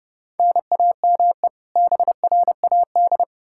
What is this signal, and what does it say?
Morse code, heard as a keyed tone, about 30 wpm, 700 Hz. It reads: NAME BRAD
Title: Practice QSO - Exchange 1 - Beginning - 30wpm